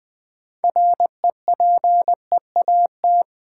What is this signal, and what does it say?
Morse code, heard as a keyed tone, about 20 wpm, 700 Hz. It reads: REPEAT
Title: Sentences from Top 1600 Words - New Word Drill - 20wpm